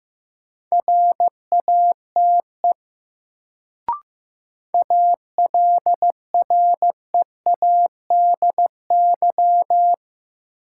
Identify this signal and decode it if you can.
Morse code, heard as a keyed tone, about 15 wpm, 700 Hz. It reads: RATE ALREADY
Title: Sentences from Top 700 Words - New Word Drill - 15wpm